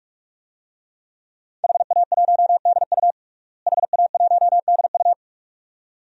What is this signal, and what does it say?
Morse code, heard as a keyed tone, about 46 wpm, 700 Hz. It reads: HA1DU HA1DU
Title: DXCC-100 Call Signs - 5 Characters - Repeated 2x - 45wpm